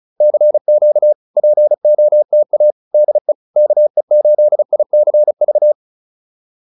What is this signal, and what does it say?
Morse code, heard as a keyed tone, about 37 wpm, 600 Hz. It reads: CQ POTA DE KE8ICV
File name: POTA103 35